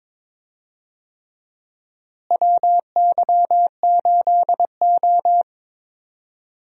Morse code, heard as a keyed tone, about 22 wpm, 700 Hz. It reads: WY8O
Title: USA Call Signs - 4 Characters - Repeated 3x - 22wpm